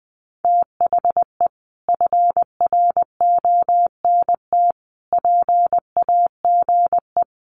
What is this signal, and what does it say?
Morse code, heard as a keyed tone, about 20 wpm, 700 Hz. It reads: THE FRONT PAGE
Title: Sets of 5 Words Encore 20wpm